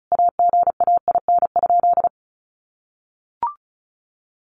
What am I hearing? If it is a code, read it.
Morse code, heard as a keyed tone, about 34 wpm, 700 Hz. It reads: AGAIN?
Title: Sentences from Top 1400 Words 35wpm